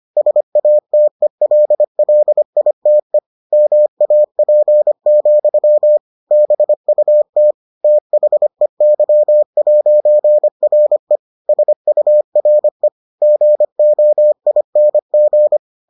Morse code, heard as a keyed tone, about 25 wpm, 600 Hz. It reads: SATELLITE MAP, BUT THEY'RE SURE GOING